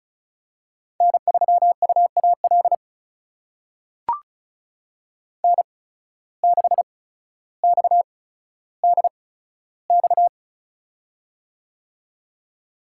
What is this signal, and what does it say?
Morse code, heard as a keyed tone, about 36 wpm, spaced as for 15 wpm, 700 Hz. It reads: N3UAL N6XDX
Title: USA Call Signs - 5 Characters - 8x Character Spacing - 35wpm